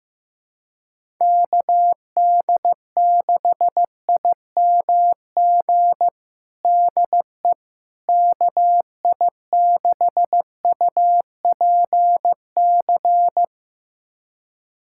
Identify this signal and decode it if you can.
Morse code, heard as a keyed tone, about 15 wpm, 700 Hz. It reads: KD6IMG DE KI6UPC
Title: USA Callsign DE Callsign 15wpm